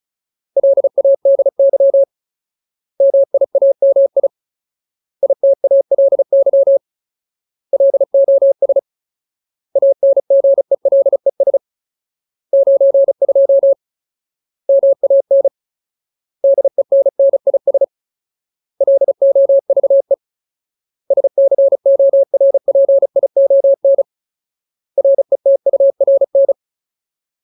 Morse code, heard as a keyed tone, about 34 wpm, 550 Hz. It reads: LADY MIAMI ITALY LOS ANGELES 92 MAN DENNIS LOVE SCORPION RETURN
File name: Ultimate Word List / 35WPM Ultimate